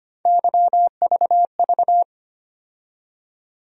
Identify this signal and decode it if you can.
Morse code, heard as a keyed tone, about 25 wpm, 700 Hz. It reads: YVV
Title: USA Call Sign Postfixes - 3 Letters 25wpm